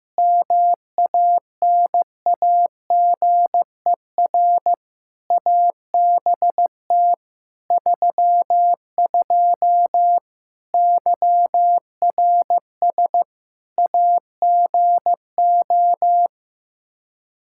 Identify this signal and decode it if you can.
Morse code, heard as a keyed tone, about 16 wpm, 700 Hz. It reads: MANAGER ABT 32 YRS AGO
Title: Practice QSO - Exchange 3 - More Info - 15wpm